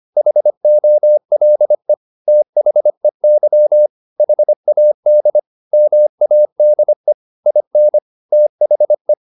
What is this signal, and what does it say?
Morse code, heard as a keyed tone, about 25 wpm, 600 Hz. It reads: HOLE THEY HAD MADE IN THE